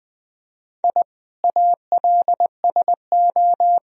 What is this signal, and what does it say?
Morse code, heard as a keyed tone, about 20 wpm, 700 Hz. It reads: I ALSO
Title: Sentences from Top 1000 Words - Review Part 1 - 20wpm